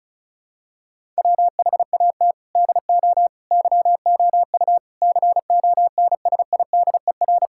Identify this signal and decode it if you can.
Morse code, heard as a keyed tone, about 34 wpm, 690 Hz. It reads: WHAT DO YOU CONSIDER
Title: Sentences from Top 1000 Words - Review Part 5 - 35wpm